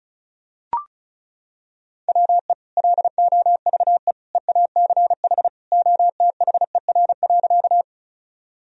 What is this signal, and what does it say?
Morse code, heard as a keyed tone, about 34 wpm, 700 Hz. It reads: WE LOVE EACH OTHER.
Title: Sentences from Top 1000 Words - Review Part 1 - 35wpm